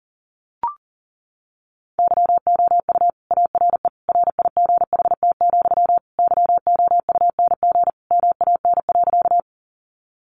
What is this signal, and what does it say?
Morse code, heard as a keyed tone, about 40 wpm, 700 Hz. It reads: YOU ARE RIGHT, YOUNG MAN.
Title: Sentences from Top 400 Words 40wpm